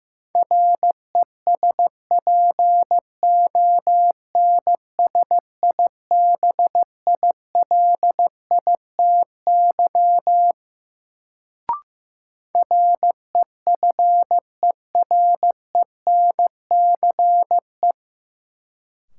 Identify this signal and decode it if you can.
Morse code, heard as a keyed tone, about 16 wpm, 700 Hz. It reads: RESPONSIBILITY REFERENCE
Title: Sentences from Top 1200 Words - New Word Drill Alt - 15wpm